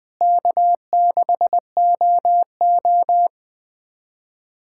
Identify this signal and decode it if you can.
Morse code, heard as a keyed tone, about 20 wpm, 700 Hz. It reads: K6OO
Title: USA Callsign DE Callsign - 4 Characters - 20wpm